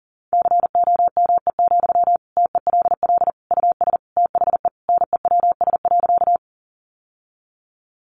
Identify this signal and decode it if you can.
Morse code, heard as a keyed tone, about 40 wpm, 700 Hz. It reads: COME, TELL US THE NEWS.
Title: Sentences from Top 800 Words 40wpm